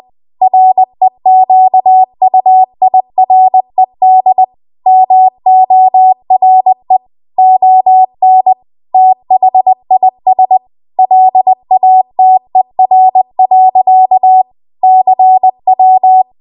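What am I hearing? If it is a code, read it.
Morse code, heard as a keyed tone, about 20 wpm, 750 Hz. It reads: REQUIRED MORE ON THIS LATER. CW
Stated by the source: Morse code practice at 20 WPM.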